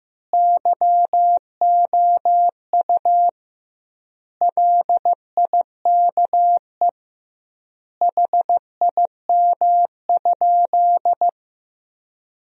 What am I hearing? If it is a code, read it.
Morse code, heard as a keyed tone, about 15 wpm, 700 Hz. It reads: YOU LIKE HIM?
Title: 2x Word-Spacing - Sentences from Top 100 Words - 15wpm